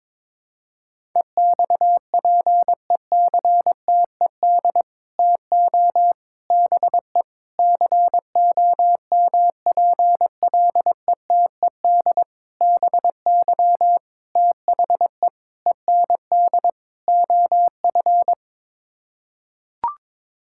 Morse code, heard as a keyed tone, about 22 wpm, 700 Hz. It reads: EXPECTED TO BE COMPLETED BY THE END OF
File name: practice / 22wpm / Sets of 8 Words - 22wpm